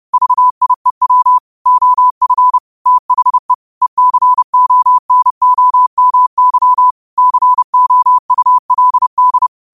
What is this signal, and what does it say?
Morse code, heard as a keyed tone, about 30 wpm, 1000 Hz. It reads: VIEW OF THE ECONOMY COULD